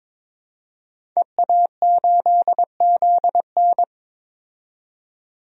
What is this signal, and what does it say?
Morse code, heard as a keyed tone, about 22 wpm, 700 Hz. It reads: EA8ZN
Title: DXCC-100 Call Signs - 5 Characters - 22wpm